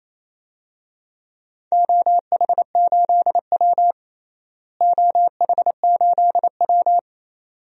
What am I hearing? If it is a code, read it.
Morse code, heard as a keyed tone, about 28 wpm, 700 Hz. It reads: OH8W OH8W
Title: DXCC-100 Call Signs - 4 Characters - Repeated 2x - 28wpm